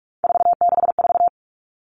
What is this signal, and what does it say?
Morse code, heard as a keyed tone, about 44 wpm, 700 Hz. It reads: 464
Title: Numbers - 3-Digit 45wpm